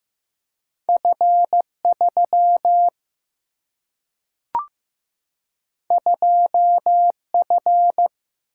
Morse code, heard as a keyed tone, about 15 wpm, 700 Hz. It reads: F 3 2 F
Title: Single-Letter Single-Number 15wpm